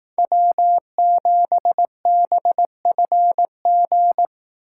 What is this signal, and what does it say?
Morse code, heard as a keyed tone, about 18 wpm, 700 Hz. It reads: W7BFG
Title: Speed Racing - USA 1x3 Call Signs 15wpm